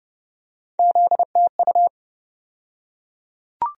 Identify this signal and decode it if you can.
Morse code, heard as a keyed tone, about 31 wpm, 700 Hz. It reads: ZTU
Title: USA Call Sign Postfixes - 3 Letters 30wpm